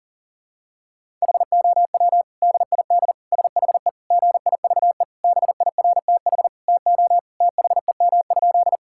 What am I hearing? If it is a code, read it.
Morse code, heard as a keyed tone, about 40 wpm, 700 Hz. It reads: HOW DID SHE GIVE BIRTH TO THEM?
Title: Sentences from Top 1000 Words - Review Part 5 - 40wpm